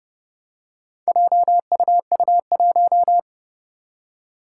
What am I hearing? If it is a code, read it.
Morse code, heard as a keyed tone, about 30 wpm, 700 Hz. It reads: JUU1
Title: Four Challenging Letter-Number Combinations 30wpm